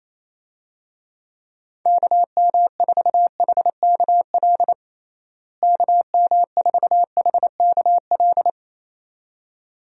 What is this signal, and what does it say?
Morse code, heard as a keyed tone, about 28 wpm, 700 Hz. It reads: KM4HKL KM4HKL
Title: USA Call Signs - 6 Characters - Repeated 2x - 28wpm